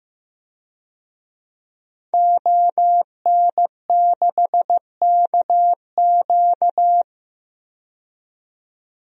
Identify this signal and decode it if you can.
Morse code, heard as a keyed tone, about 15 wpm, 700 Hz. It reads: ON6KQ
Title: DXCC-100 Call Signs - 5 Characters - 15wpm